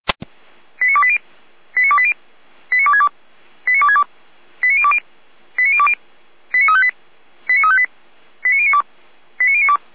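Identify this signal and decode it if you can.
Five-tone selective call (ZVEI), heard as quick runs of five tones, tones of about 70 ms. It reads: 89280 89280 89262 89262 89020 89020 89358 89358 89092 89092
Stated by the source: ZVEI five-tone selective call received by SDR, code 89230, archived by Feuerwehr Orlamünde.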